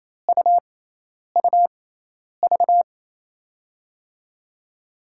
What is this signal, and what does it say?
Morse code, heard as a keyed tone, about 28 wpm, spaced as for 10 wpm, 700 Hz. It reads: UUV
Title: USA Call Signs - 6x Character Spacing - 28wpm